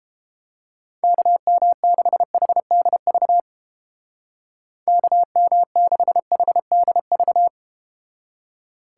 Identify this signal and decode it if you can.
Morse code, heard as a keyed tone, about 31 wpm, 700 Hz. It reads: KM6HDV KM6HDV